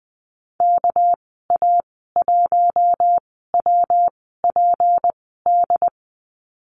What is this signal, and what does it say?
Morse code, heard as a keyed tone, about 20 wpm, 700 Hz. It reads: K A 1 W P D
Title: 207 Lesson 33g - Letter j - Call Signs 20wpm